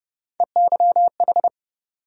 Morse code, heard as a keyed tone, about 29 wpm, 700 Hz. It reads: EYH
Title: USA Call Sign Postfixes - 3 Letters 30wpm